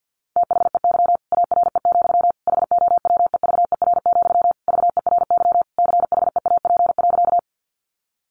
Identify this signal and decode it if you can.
Morse code, heard as a keyed tone, about 50 wpm, 700 Hz. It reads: THEY ARE, HOWEVER, VERY CHEAP.
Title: Sentences from Top 1200 Words 50wpm